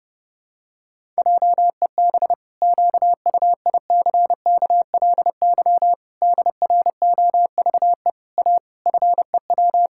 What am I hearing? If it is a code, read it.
Morse code, heard as a keyed tone, about 30 wpm, 700 Hz. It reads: JEB QUICKLY DROVE A FEW
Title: Pangram Challenge v2 - 30wpm